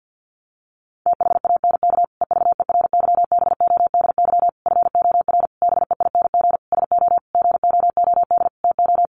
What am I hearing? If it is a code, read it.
Morse code, heard as a keyed tone, about 50 wpm, 700 Hz. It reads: THANK EVERYBODY FOR BEING SO GOOD TO
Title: Sentences from Top 1400 Words 50wpm